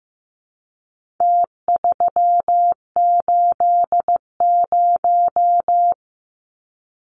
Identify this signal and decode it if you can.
Morse code, heard as a keyed tone, about 15 wpm, 700 Hz. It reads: T380
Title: Numbers - Contest Style - 4-Digit 15wpm